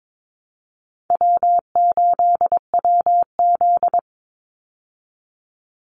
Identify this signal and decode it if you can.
Morse code, heard as a keyed tone, about 22 wpm, 700 Hz. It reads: W8WZ
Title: Easy Call Signs Repeated 3x - 22wpm